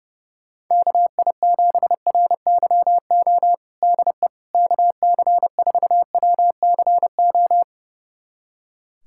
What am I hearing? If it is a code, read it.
Morse code, heard as a keyed tone, about 30 wpm, 700 Hz. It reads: KI7RYO DE KC4WCO
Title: USA Callsign DE Callsign - 6 Characters - 30wpm